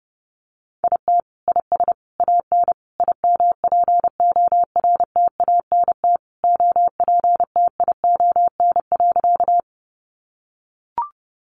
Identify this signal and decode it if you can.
Morse code, heard as a keyed tone, about 30 wpm, 700 Hz. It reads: IT IS AN IMPORTANT OPTION.
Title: Sentences from Top 1300 Words 30wpm